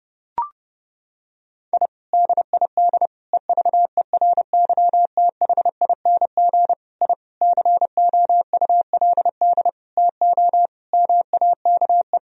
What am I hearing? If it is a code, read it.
Morse code, heard as a keyed tone, about 32 wpm, 700 Hz. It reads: I DID EVERYTHING I COULD TO MAKE
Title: Sentences from Top 1000 Words - Review Part 1 - 30wpm